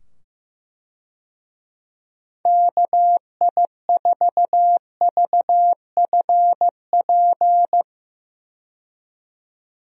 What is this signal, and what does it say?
Morse code, heard as a keyed tone, about 15 wpm, 700 Hz. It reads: KI4VFP
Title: USA Call Signs - 6 Characters - 15wpm